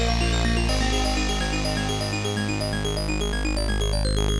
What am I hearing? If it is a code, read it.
There is no code.